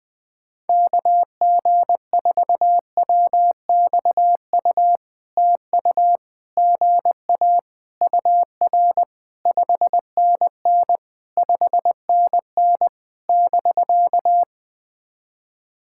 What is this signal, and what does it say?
Morse code, heard as a keyed tone, about 20 wpm, 700 Hz. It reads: KG4WXU TU GA UR 5NN 5NN <BK>
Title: POTA QSO Practice - 20wpm